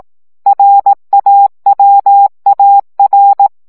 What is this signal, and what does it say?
Morse code, heard as a keyed tone, about 18 wpm, 800 Hz. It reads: RAWAR
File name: CW gyakorlas a r w 15 baud